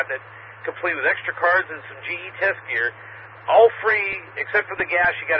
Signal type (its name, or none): Emergency Alert System (SAME) tones